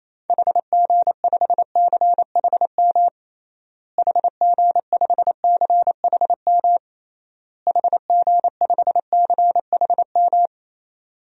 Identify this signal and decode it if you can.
Morse code, heard as a keyed tone, about 28 wpm, 700 Hz. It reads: HG5CHM HG5CHM HG5CHM
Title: DXCC-100 Call Signs - 6 Characters - Repeated 3x - 28wpm